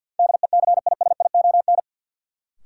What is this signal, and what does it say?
Morse code, heard as a keyed tone, about 52 wpm, 700 Hz. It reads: DECISION
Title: Sentences from Top 1000 Words - New Word Drill - 50wpm